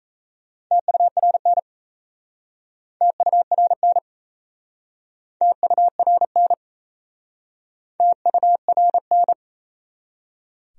Morse code, heard as a keyed tone, about 37 wpm, 700 Hz. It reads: TURN TURN TURN TURN